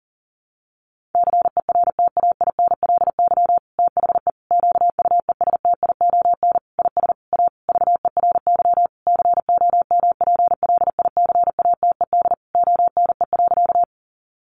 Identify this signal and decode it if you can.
Morse code, heard as a keyed tone, about 40 wpm, 700 Hz. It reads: CERTAINLY THE QUESTION IS A VERY COMPLICATED ONE.
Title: Sentences from Top 1200 Words 40wpm